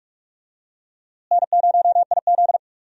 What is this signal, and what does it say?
Morse code, heard as a keyed tone, about 46 wpm, 700 Hz. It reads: N0IZ
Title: USA Call Signs - 4 Characters - Repeated 3x - 45wpm